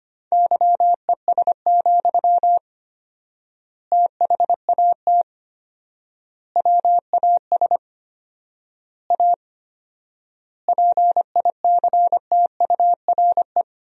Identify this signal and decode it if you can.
Morse code, heard as a keyed tone, about 25 wpm, 700 Hz. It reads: YES, THAT WAS A PICTURE
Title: 4x Word-Spacing - Sentences from Top 500 Words - 25wpm